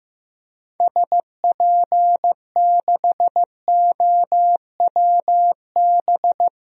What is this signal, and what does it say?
Morse code, heard as a keyed tone, about 15 wpm, 700 Hz. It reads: SP6OWB
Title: DXCC-100 Call Signs - 6 Characters - Repeated 2x - 15wpm